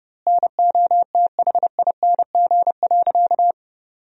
Morse code, heard as a keyed tone, about 32 wpm, 700 Hz. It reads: NOTHING.
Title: 4x Word-Spacing - Sentences from Top 500 Words - 30wpm